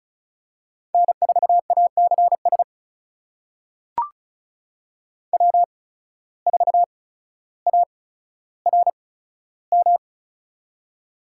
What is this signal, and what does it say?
Morse code, heard as a keyed tone, about 36 wpm, spaced as for 15 wpm, 700 Hz. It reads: N4ACS W4ARM
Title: USA Call Signs - 5 Characters - 8x Character Spacing - 35wpm